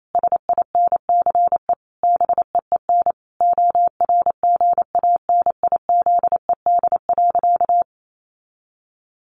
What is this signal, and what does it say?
Morse code, heard as a keyed tone, about 28 wpm, 700 Hz. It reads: SINCE BEEN ORGANIZED.
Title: Sentences from Top 1400 Words 28wpm